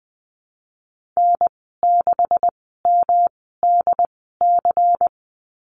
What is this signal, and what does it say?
Morse code, heard as a keyed tone, about 20 wpm, spaced as for 14 wpm, 700 Hz. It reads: N6MDC
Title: 128 Lesson 21g - Number 6 - Call Signs 20wpm